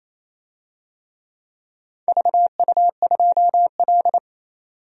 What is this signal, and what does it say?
Morse code, heard as a keyed tone, about 28 wpm, 700 Hz. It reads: VU2L
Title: DXCC-100 Call Signs - 4 Characters - 28wpm